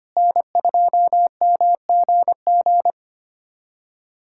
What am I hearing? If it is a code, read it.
Morse code, heard as a keyed tone, about 25 wpm, 700 Hz. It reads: N2MGG
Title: USA Callsign DE Callsign 25wpm